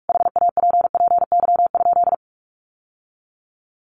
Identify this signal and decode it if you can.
Morse code, heard as a keyed tone, about 44 wpm, 700 Hz. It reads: HAPPY?
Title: Sentences from Top 1100 Words 45wpm